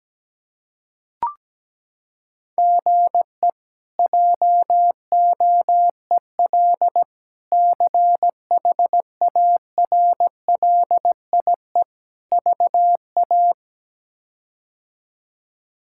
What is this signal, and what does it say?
Morse code, heard as a keyed tone, about 17 wpm, 700 Hz. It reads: GE JOEL CHARLIE VA
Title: K1USN SST QSO Practice - 17wpm